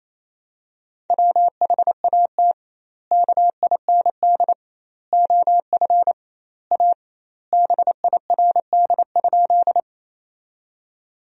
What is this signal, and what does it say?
Morse code, heard as a keyed tone, about 28 wpm, 700 Hz. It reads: WHAT KIND OF A BIRD?
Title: 2x Word-Spacing - Sentences from Top 400 Words - 28wpm